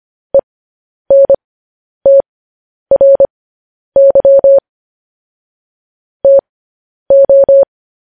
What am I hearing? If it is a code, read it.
Morse code, heard as a keyed tone, about 25 wpm, spaced as for 10 wpm, 550 Hz. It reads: ENTRY TO
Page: news headlines in morse code at 10 wpm